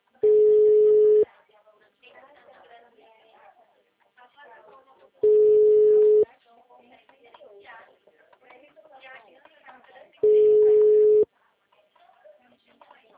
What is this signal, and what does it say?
Ringback tone (Europe, 425 Hz), with speech.
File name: out- -2011- - - .23724